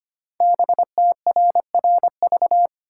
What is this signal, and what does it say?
Morse code, heard as a keyed tone, about 25 wpm, 700 Hz. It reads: BTRRV
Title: Code Groups - 5 Letters - 25wpm